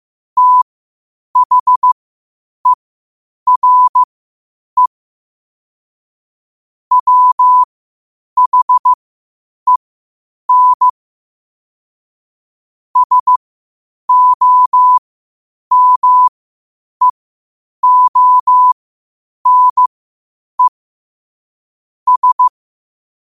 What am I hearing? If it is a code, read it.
Morse code, heard as a keyed tone, about 15 wpm, spaced as for 8 wpm, 1000 Hz. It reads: THERE WHEN SOMEONE S
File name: qotdfiles / 8wpm